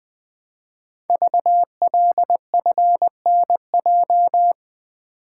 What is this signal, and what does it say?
Morse code, heard as a keyed tone, about 20 wpm, 700 Hz. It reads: VLFNJ
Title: Code Groups - 5 Letters - 20wpm